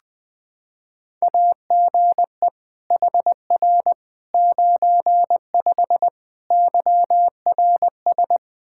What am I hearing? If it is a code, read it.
Morse code, heard as a keyed tone, about 20 wpm, 700 Hz. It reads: AGE HR 95 YRS
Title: Practice QSO - Exchange 3 - More Info - 20wpm